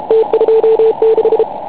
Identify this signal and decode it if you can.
Morse code, heard as a keyed tone, about 33 wpm, 440 Hz. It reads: T26
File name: Pile UP CUC204